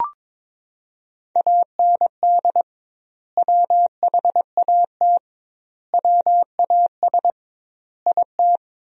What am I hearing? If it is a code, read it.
Morse code, heard as a keyed tone, about 22 wpm, 700 Hz. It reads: AND WHAT WAS IT
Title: 2x Word-Spacing - Sentences from Top 100 Words - 22wpm